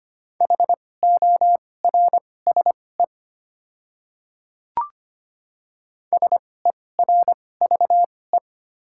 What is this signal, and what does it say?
Morse code, heard as a keyed tone, about 25 wpm, spaced as for 18 wpm, 700 Hz. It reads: HORSE SERVE
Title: 2x Character Spacing - Top 500 Words - 25wpm